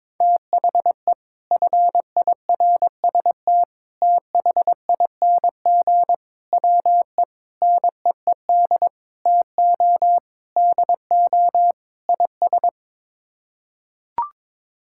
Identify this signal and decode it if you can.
Morse code, heard as a keyed tone, about 22 wpm, 700 Hz. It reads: THE FIRST THING WE NEED TO DO IS
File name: practice / 22wpm / Sets of 8 Words - 22wpm